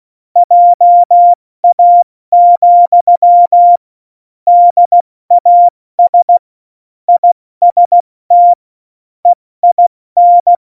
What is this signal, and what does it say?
Morse code, heard as a keyed tone, about 16 wpm, 700 Hz. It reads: JA, DAS IST EIN
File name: Morsekurs / Witze / Witz 12 Mai, 2020 70 F80